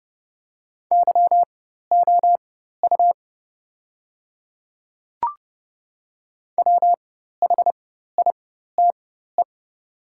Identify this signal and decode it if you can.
Morse code, heard as a keyed tone, about 30 wpm, spaced as for 14 wpm, 700 Hz. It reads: YOU WHITE
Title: 4x Character Spacing - Top 300 Words - 30wpm